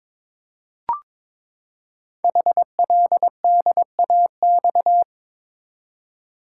Morse code, heard as a keyed tone, about 22 wpm, 700 Hz. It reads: HLDAX
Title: Code Groups - 5 Letters - 22wpm